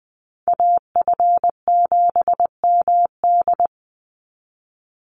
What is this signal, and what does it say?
Morse code, heard as a keyed tone, about 20 wpm, 700 Hz. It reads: AF7MD
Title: NA SOTA Chaser Callsigns 20wpm